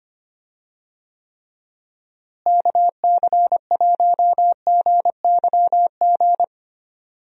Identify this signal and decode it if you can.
Morse code, heard as a keyed tone, about 25 wpm, 700 Hz. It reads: KC1GYG
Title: USA Call Signs - 6 Characters - Repeated 2x - 25wpm